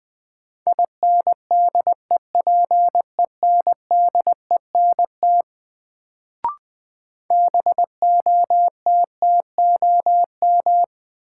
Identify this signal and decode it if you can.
Morse code, heard as a keyed tone, about 21 wpm, 700 Hz. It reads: INDEPENDENT BOTTOM
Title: Sentences from Top 900 Words - New Word Drill - 20wpm